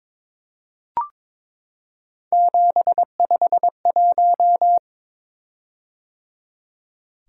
Morse code, heard as a keyed tone, about 22 wpm, 700 Hz. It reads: 751
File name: Three Letter-Number Combinations - 22wpm